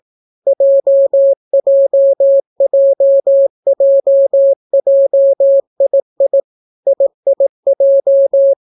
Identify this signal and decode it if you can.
Morse code, heard as a keyed tone, about 18 wpm, 550 Hz. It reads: JJJJJII IIJ